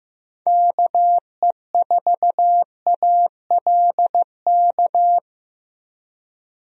Morse code, heard as a keyed tone, about 16 wpm, 700 Hz. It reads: KE4ALK
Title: USA Call Signs - 6 Characters - Repeated 3x - 15wpm